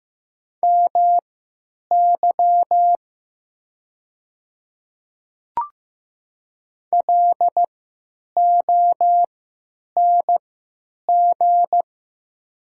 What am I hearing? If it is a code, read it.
Morse code, heard as a keyed tone, about 15 wpm, spaced as for 9 wpm, 700 Hz. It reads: MY LONG
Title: 3x Character Spacing - Top 200 Words - 15wpm